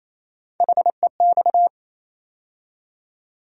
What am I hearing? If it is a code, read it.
Morse code, heard as a keyed tone, about 28 wpm, 700 Hz. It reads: HEX
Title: USA Call Sign Postfixes - 3 Letters 28wpm